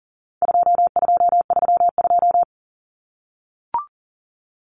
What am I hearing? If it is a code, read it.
Morse code, heard as a keyed tone, about 40 wpm, 700 Hz. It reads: 2232
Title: Common Cut Numbers - 4-Digit 40wpm